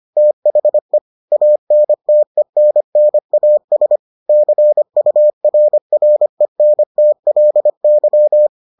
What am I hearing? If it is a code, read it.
Morse code, heard as a keyed tone, about 25 wpm, 600 Hz. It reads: THE ANTENNAS CURRENTLY